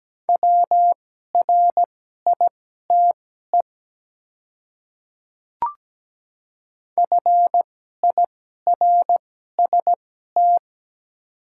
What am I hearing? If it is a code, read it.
Morse code, heard as a keyed tone, about 17 wpm, spaced as for 12 wpm, 700 Hz. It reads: WRITE FIRST
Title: 2x Character Spacing - Top 200 Words - 17wpm